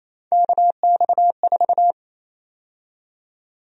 Morse code, heard as a keyed tone, about 28 wpm, 700 Hz. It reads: KX4
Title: USA Call Sign Prefixes - 28wpm